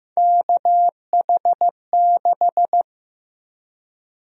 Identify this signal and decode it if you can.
Morse code, heard as a keyed tone, about 15 wpm, 700 Hz. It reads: KH6
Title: USA Call Sign Prefixes - Repeated 2x - 15wpm